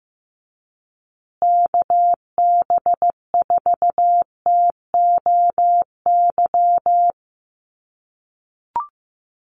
Morse code, heard as a keyed tone, about 15 wpm, 700 Hz. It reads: KB4TOY
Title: USA Call Signs Repeated 2x 15wpm